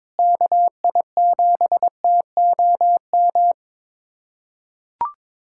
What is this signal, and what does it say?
Morse code, heard as a keyed tone, about 21 wpm, 700 Hz. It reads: KI7TOM
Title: USA Callsign DE Callsign - 6 Characters - 22wpm